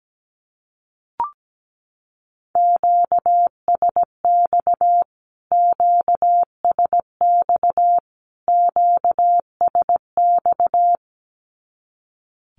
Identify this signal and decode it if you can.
Morse code, heard as a keyed tone, about 17 wpm, 700 Hz. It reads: QSX QSX QSX
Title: Q-Codes Repeated 3x 17wpm